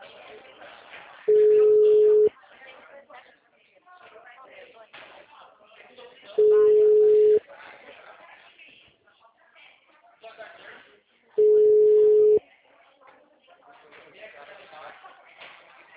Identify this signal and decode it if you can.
Ringback tone (Europe, 425 Hz), with speech.
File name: out- -2014- - - .45703